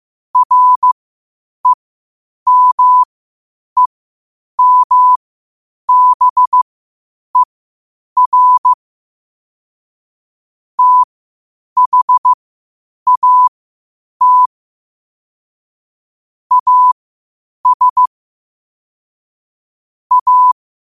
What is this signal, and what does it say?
Morse code, heard as a keyed tone, about 15 wpm, spaced as for 8 wpm, 1000 Hz. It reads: REMEMBER THAT AS A
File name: qotdfiles / 8wpm